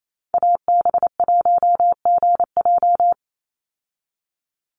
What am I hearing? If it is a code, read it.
Morse code, heard as a keyed tone, about 28 wpm, 690 Hz. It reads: AB1GJ
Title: Call Signs Repeated 3x - 28wpm